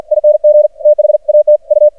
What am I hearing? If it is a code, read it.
Morse code, heard as a keyed tone, about 51 wpm, 600 Hz. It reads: ING THAT U
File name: archived / CRSnet